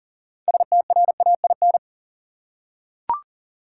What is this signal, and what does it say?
Morse code, heard as a keyed tone, about 40 wpm, 700 Hz. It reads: STRAIN
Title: Six Letter Words - 40wpm